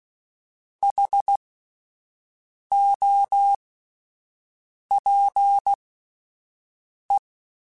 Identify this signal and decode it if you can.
Morse code, heard as a keyed tone, about 16 wpm, spaced as for 5 wpm, 800 Hz. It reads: HOPE
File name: Alphabetical / Alphabetical mp3s / Hs @5